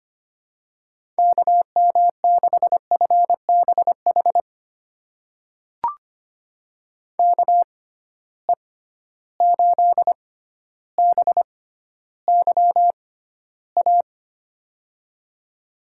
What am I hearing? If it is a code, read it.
Morse code, heard as a keyed tone, about 25 wpm, spaced as for 13 wpm, 700 Hz. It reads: KM6FBH KE8BYA
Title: USA Call Signs - 6 Characters - 6x Character Spacing - 25wpm